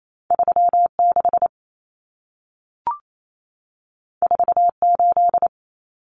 Morse code, heard as a keyed tone, about 28 wpm, 700 Hz. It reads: 36 48
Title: Common Cut Numbers - 2-Digit 28wpm